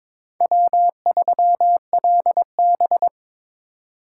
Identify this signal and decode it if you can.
Morse code, heard as a keyed tone, about 25 wpm, 700 Hz. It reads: W3LB
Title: Speed Racing - USA 1x2 Call Signs 22wpm